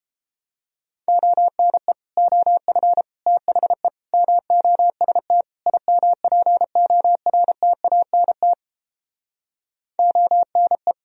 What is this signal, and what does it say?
Morse code, heard as a keyed tone, about 29 wpm, 700 Hz. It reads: ONE OF THE MOST IMPORTANT ONE